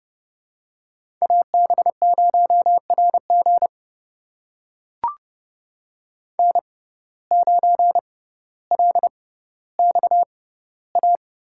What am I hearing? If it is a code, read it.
Morse code, heard as a keyed tone, about 30 wpm, spaced as for 15 wpm, 700 Hz. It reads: AB0RG N9LXA
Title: USA Call Signs - 5 Characters - 6x Character Spacing - 30wpm